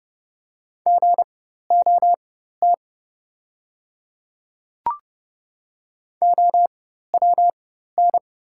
Morse code, heard as a keyed tone, about 30 wpm, spaced as for 14 wpm, 700 Hz. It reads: GOT OWN
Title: 4x Character Spacing - Top 300 Words - 30wpm